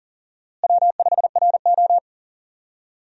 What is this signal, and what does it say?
Morse code, heard as a keyed tone, about 38 wpm, 700 Hz. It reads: W5RO
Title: USA Call Signs - 4 Characters - Repeated 2x - 40wpm